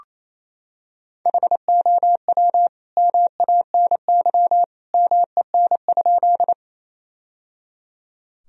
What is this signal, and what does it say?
Morse code, heard as a keyed tone, about 28 wpm, 700 Hz. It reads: HOW MANY MEN?
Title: Sentences from Top 1000 Words - Review Part 1 - 28wpm